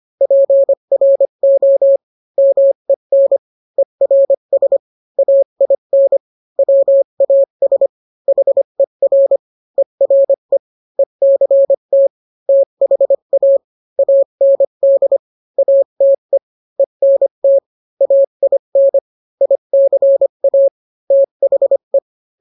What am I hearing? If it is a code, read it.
Morse code, heard as a keyed tone, about 25 wpm, 550 Hz. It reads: PRO MEN ERS AIN WAS HER ERE ECT THA AND ATE ENT AIN ICA THE